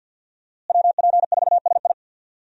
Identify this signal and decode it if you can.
Morse code, heard as a keyed tone, about 51 wpm, 700 Hz. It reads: WP4SI
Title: USA Call Signs - 5 Characters - Repeated 3x - 50wpm